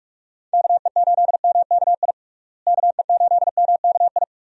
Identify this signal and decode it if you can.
Morse code, heard as a keyed tone, about 46 wpm, 700 Hz. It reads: KE8MKI KE8MKI
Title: USA Call Signs - 6 Characters - Repeated 3x - 45wpm